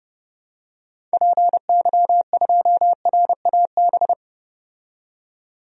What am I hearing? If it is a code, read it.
Morse code, heard as a keyed tone, about 30 wpm, 700 Hz. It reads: PY2RAB
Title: DXCC-100 Call Signs - 6 Characters - 30wpm